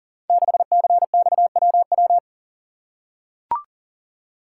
Morse code, heard as a keyed tone, about 42 wpm, 700 Hz. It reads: 6CXWW
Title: Code Groups - 5 Letter:Numbers - 40wpm